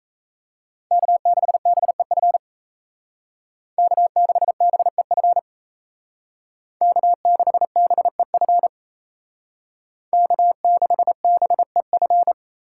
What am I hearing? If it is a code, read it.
Morse code, heard as a keyed tone, about 33 wpm, 700 Hz. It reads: K6BEF K6BEF K6BEF K6BEF
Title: Speed Racing - USA 1x3 Call Signs 28wpm